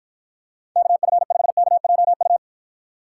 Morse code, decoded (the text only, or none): KF5KJU